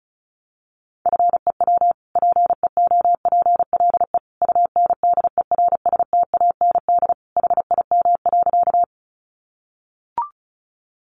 Morse code, heard as a keyed tone, about 34 wpm, 700 Hz. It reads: FEW PEOPLE UNDERSTAND HIM.
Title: Sentences from Top 600 Words 35wpm